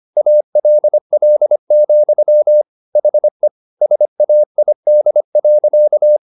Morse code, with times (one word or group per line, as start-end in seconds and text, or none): 0.2-2.6 ALL,
3.0-3.5 HE
3.8-6.2 SAID.